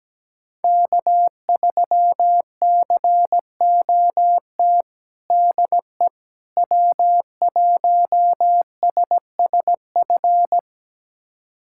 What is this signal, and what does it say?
Morse code, heard as a keyed tone, about 17 wpm, 700 Hz. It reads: K3COT DE W1SSF
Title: USA Callsign DE Callsign 17wpm